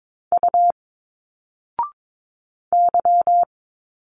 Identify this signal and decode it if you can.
Morse code, heard as a keyed tone, about 22 wpm, 700 Hz. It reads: U Y
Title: Single Letters 22wpm